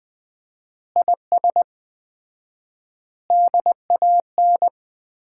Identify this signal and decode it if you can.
Morse code, heard as a keyed tone, about 20 wpm, 700 Hz. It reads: IS DAN
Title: Common QSO Phrases - 4x Word-Spacing - 20wpm